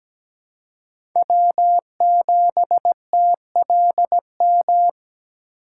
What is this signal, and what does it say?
Morse code, heard as a keyed tone, about 17 wpm, 700 Hz. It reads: W7TLM
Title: USA Call Signs - 5 Characters - 2x Character Spacing - 17wpm